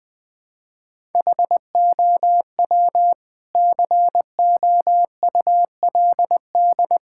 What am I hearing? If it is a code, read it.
Morse code, heard as a keyed tone, about 19 wpm, 700 Hz. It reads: HOW COULD
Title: Sentences from Top 1000 Words - Review Part 6 - 20wpm